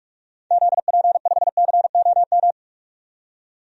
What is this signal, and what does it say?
Morse code, heard as a keyed tone, about 44 wpm, 700 Hz. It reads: ZP5COM
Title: DXCC-100 Call Signs - 6 Characters - 45wpm